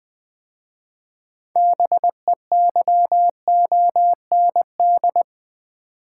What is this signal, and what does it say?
Morse code, heard as a keyed tone, about 20 wpm, 700 Hz. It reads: BEYOND
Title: 6 Letter Words - First 3 Letters Given - 20wpm